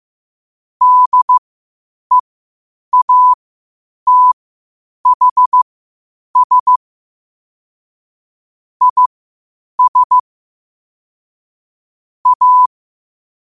Morse code, heard as a keyed tone, about 15 wpm, spaced as for 8 wpm, 1000 Hz. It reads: DEATHS IS A
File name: qotdfiles / 8wpm